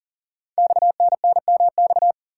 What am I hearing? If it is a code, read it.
Morse code, heard as a keyed tone, about 42 wpm, 700 Hz. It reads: XNNMX
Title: Code Groups - 5 Letters - 40wpm